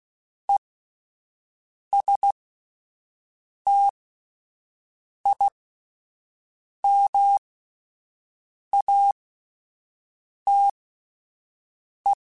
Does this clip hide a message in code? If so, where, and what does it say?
Yes: Morse code from 0.5 to 12.1 s, heard as a keyed tone, about 16 wpm, spaced as for 5 wpm, 800 Hz. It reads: ESTIMATE